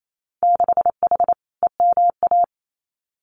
Morse code, heard as a keyed tone, about 28 wpm, 700 Hz. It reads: 6H EMA
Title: Winter Field Day Exchanges Encore 28wpm